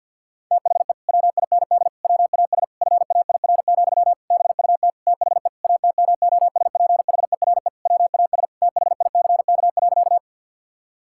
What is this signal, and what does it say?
Morse code, heard as a keyed tone, about 50 wpm, 700 Hz. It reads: THE WIND WAS FAIR, BUT THE ATMOSPHERE WAS THICK.
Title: Sentences from Top 1600 Words 50wpm